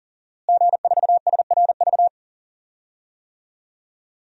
Morse code, heard as a keyed tone, about 40 wpm, 700 Hz. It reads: G4SRV
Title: DXCC-100 Call Signs - 5 Characters - Repeated 3x - 40wpm